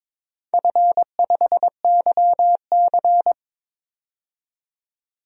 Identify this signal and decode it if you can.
Morse code, heard as a keyed tone, about 22 wpm, 700 Hz. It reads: F5YC
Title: DXCC-100 Call Signs - 4 Characters - 22wpm